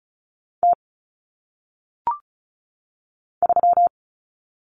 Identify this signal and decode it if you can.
Morse code, heard as a keyed tone, about 36 wpm, 700 Hz. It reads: T 3
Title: Common Cut Numbers - 1-Digit 35wpm